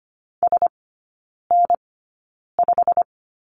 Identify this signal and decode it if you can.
Morse code, heard as a keyed tone, about 25 wpm, spaced as for 9 wpm, 690 Hz. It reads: SN5
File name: Lesson 15e - Number 5 - Three Characters 25wpm